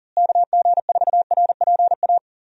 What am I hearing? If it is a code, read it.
Morse code, heard as a keyed tone, about 40 wpm, 700 Hz. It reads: KG4RPA
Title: USA Call Signs - 6 Characters - Repeated 3x - 40wpm